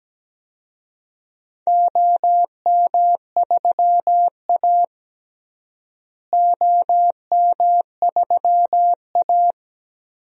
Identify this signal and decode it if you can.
Morse code, heard as a keyed tone, about 17 wpm, 700 Hz. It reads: OM3A OM3A
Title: DXCC-100 Call Signs - 4 Characters - Repeated 3x - 17wpm